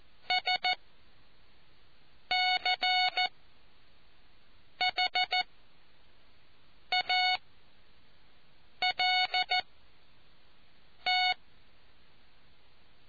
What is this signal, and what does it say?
Morse code, heard as a keyed tone, about 14 wpm, spaced as for 5 wpm, 740 Hz. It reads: SCHALT